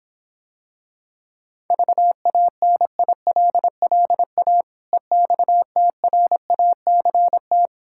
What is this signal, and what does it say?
Morse code, heard as a keyed tone, about 26 wpm, 700 Hz. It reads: VANILLA EXTRACT